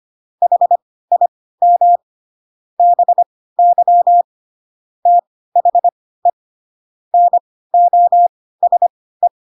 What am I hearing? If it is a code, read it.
Morse code, heard as a keyed tone, about 25 wpm, spaced as for 16 wpm, 700 Hz. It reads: HIM BY THE NOSE
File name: Pinocchio / Pinocchio ch3 10 25 ik1hge